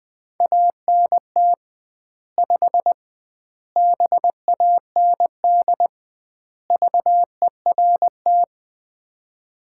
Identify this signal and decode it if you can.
Morse code, heard as a keyed tone, about 20 wpm, 700 Hz. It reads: ANT 5 BAND VERT
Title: Common QSO Phrases - 2x Word-Spacing - 20wpm